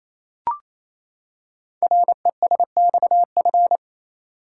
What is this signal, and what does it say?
Morse code, heard as a keyed tone, about 27 wpm, 700 Hz. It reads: RESXF
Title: Code Groups - 5 Letters - 28wpm